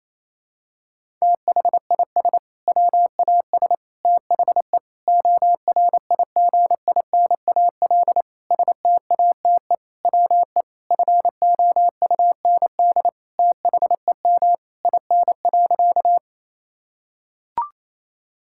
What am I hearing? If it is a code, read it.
Morse code, heard as a keyed tone, about 28 wpm, 700 Hz. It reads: THIS WAS THE ORIGINAL STATE WE FOUND THEM IN.
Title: Sentences from Top 1000 Words - Review Part 6 - 28wpm